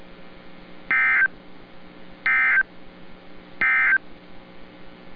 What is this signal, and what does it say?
Emergency Alert System (SAME) tones, heard as harsh buzzing bursts. It is heard from 0.8 to 4.1 s.